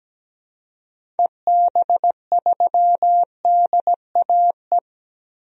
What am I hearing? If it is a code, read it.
Morse code, heard as a keyed tone, about 17 wpm, 700 Hz. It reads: EB3DAE
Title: DXCC-100 Call Signs - 6 Characters - Repeated 3x - 17wpm